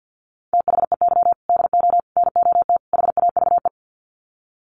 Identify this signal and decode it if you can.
Morse code, heard as a keyed tone, about 50 wpm, 700 Hz. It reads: THEY DO NOT HAVE
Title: Sets of 4 Words 50wpm